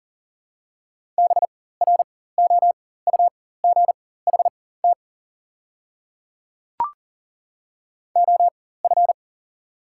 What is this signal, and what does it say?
Morse code, heard as a keyed tone, about 40 wpm, spaced as for 19 wpm, 690 Hz. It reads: BROUGHT OF
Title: 4x Character Spacing - Top 500 Words - 40wpm